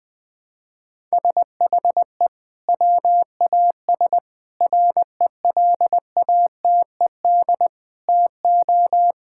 Morse code, heard as a keyed tone, about 20 wpm, 700 Hz. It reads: SHE WAS RELATED TO
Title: Sentences from Top 1000 Words - Review Part 5 - 20wpm